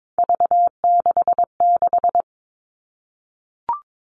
Morse code, heard as a keyed tone, about 22 wpm, 700 Hz. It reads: V66
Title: All Cut Numbers - 3-Digit 22wpm